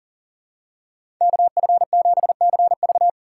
Morse code, heard as a keyed tone, about 40 wpm, 700 Hz. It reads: KF7CV
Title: USA Call Signs - 2x Character Spacing - 40wpm